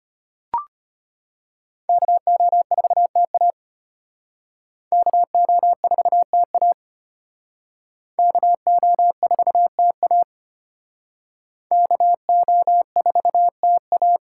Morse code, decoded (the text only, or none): KO4TA KO4TA KO4TA KO4TA